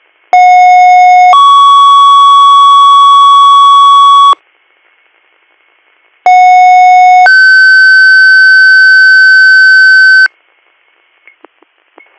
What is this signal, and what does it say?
Two-tone page, heard as a one-second tone, then a three-second tone, about 730 then 1120 Hz, about 730 then 1550 Hz. It is heard from 0.3 to 10.3 s.
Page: bradford-pa